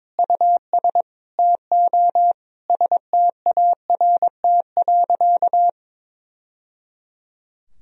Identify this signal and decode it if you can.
Morse code, heard as a keyed tone, about 22 wpm, 700 Hz. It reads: US TO START.
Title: Sentences from Top 1000 Words - Review Part 4 - 22wpm